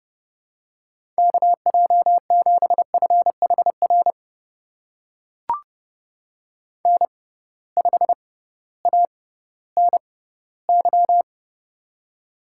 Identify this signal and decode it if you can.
Morse code, heard as a keyed tone, about 30 wpm, spaced as for 16 wpm, 700 Hz. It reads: KJ7FHR N5ANY
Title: USA Call Signs - 6x Character Spacing - 30wpm